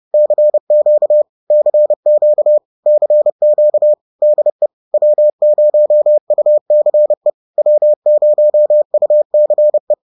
Morse code, heard as a keyed tone, about 30 wpm, 600 Hz. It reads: CQ CQ CQ DE W0UCE W0UCE
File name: 30wpm-4